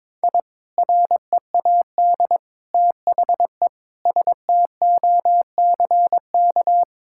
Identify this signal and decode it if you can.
Morse code, heard as a keyed tone, about 22 wpm, 700 Hz. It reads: I READ THE STOCK
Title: Sentences from Top 1000 Words - Review Part 6 - 22wpm